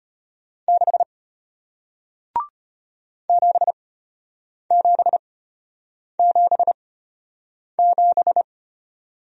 Morse code, heard as a keyed tone, about 33 wpm, spaced as for 8 wpm, 700 Hz. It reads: 6 7777